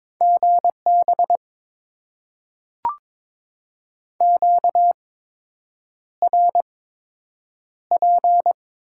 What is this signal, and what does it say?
Morse code, heard as a keyed tone, about 22 wpm, spaced as for 8 wpm, 700 Hz. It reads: GB QRP
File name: Top 90 QSO Elements - 8x Character Spacing - 22wpm